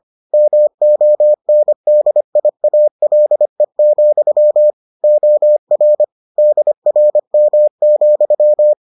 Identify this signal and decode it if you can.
Morse code, heard as a keyed tone, about 25 wpm, 600 Hz. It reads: MONDIALE, OR DRM,